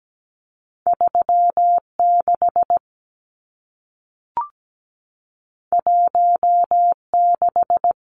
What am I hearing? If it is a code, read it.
Morse code, heard as a keyed tone, about 17 wpm, 700 Hz. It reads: 36 16
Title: Numbers - 2-Digit 17wpm